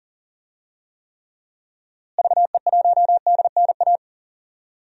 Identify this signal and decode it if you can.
Morse code, heard as a keyed tone, about 40 wpm, 700 Hz. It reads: VE1DNA
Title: DXCC-100 Call Signs - 6 Characters - Repeated 3x - 40wpm